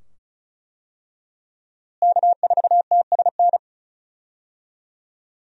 Morse code, heard as a keyed tone, about 36 wpm, 700 Hz. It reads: K4TSN
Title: USA Call Signs - 5 Characters - Repeated 2x - 35wpm